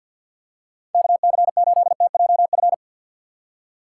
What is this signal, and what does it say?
Morse code, heard as a keyed tone, about 52 wpm, 700 Hz. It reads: KC8TJF
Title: USA Call Signs - 6 Characters - 4x Character Spacing - 50wpm